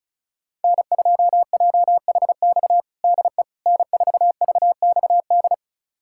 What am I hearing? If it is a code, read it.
Morse code, heard as a keyed tone, about 36 wpm, 700 Hz. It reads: N2JHX DE N4VXD
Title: USA Callsign DE Callsign - 5 Characters - 35wpm